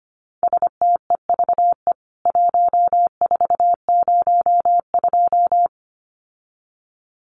Morse code, heard as a keyed tone, about 25 wpm, 700 Hz. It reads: STEVE 1402
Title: CWT Exchange 25wpm